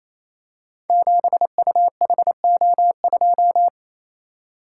Morse code, heard as a keyed tone, about 27 wpm, 700 Hz. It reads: 7UHO2
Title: Code Groups - 5 Letter:Numbers - 28wpm